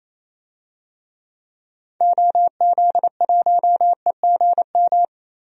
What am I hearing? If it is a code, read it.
Morse code, heard as a keyed tone, about 28 wpm, 700 Hz. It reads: OZ1EGM
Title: DXCC-100 Call Signs - 6 Characters - Repeated 3x - 28wpm